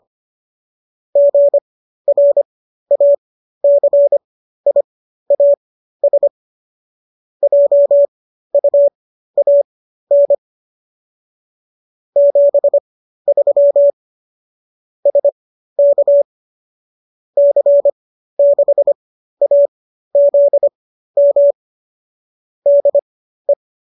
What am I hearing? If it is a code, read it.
Morse code, heard as a keyed tone, about 25 wpm, spaced as for 13 wpm, 570 Hz. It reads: GRACIAS JUAN 73 SK C6AZM DE